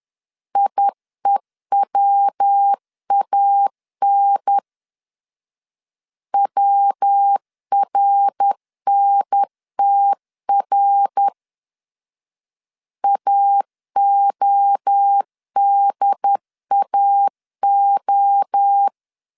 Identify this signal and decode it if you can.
Morse code, heard as a keyed tone, about 10 wpm, 800 Hz. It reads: IEWAN WRNTR AODAO
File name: CW / DET / 1 3 1500C S B 10wpm 3335